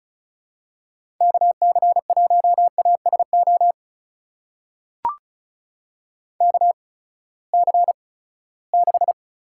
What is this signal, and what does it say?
Morse code, heard as a keyed tone, about 36 wpm, spaced as for 20 wpm, 700 Hz. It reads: KC1ASO KC6
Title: USA Call Signs - 6 Characters - 8x Character Spacing - 35wpm